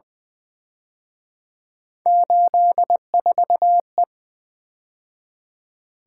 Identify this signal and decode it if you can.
Morse code, heard as a keyed tone, about 23 wpm, 700 Hz. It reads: 84E
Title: Speed-Racing - Three Letter-Number Combinations - 20wpm